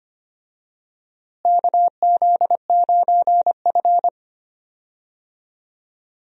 Morse code, heard as a keyed tone, about 25 wpm, 700 Hz. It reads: KZ9F
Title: USA Call Signs - 4 Characters - Repeated 2x - 25wpm